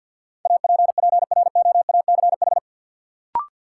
Morse code, heard as a keyed tone, about 52 wpm, 700 Hz. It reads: APPROACH
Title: Sentences from Top 1400 Words - New Word Drill Alt - 50wpm